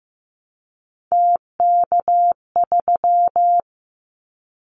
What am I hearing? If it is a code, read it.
Morse code, heard as a keyed tone, about 15 wpm, 700 Hz. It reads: TK3
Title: Callsign Prefixes for Top 100 DXCC - 15wpm